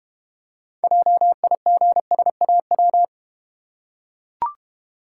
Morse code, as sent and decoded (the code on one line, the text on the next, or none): .--- .. --. ... .- .--
JIGSAW